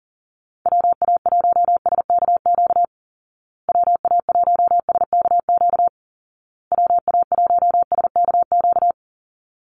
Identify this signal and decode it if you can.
Morse code, heard as a keyed tone, about 40 wpm, 700 Hz. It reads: WA1SKQ WA1SKQ WA1SKQ
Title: Call Signs Repeated 2x - 40wpm